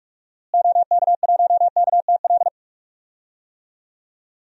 Morse code, heard as a keyed tone, about 42 wpm, 700 Hz. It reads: OK1KTL
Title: DXCC-100 Call Signs - 6 Characters - Repeated 2x - 45wpm